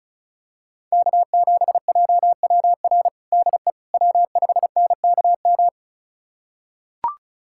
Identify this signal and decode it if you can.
Morse code, heard as a keyed tone, about 36 wpm, 700 Hz. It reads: K7JWR DE W5NKM
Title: USA Callsign DE Callsign 35wpm